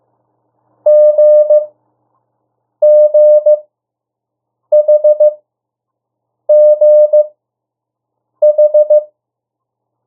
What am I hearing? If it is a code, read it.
Morse code, heard as a keyed tone, about 15 wpm, spaced as for 6 wpm, 600 Hz. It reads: GGHGH
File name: urok 1